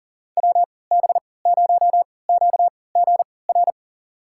Morse code, heard as a keyed tone, about 40 wpm, spaced as for 23 wpm, 700 Hz. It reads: WB0QGR
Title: USA Call Signs - 3x Character Spacing - 40wpm